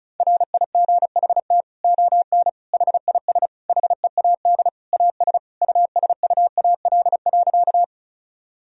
Morse code, heard as a keyed tone, about 36 wpm, 700 Hz. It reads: RIGHT ON HIS HEAD AS USUAL.
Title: Sentences from Top 1000 Words - Review Part 2 - 35wpm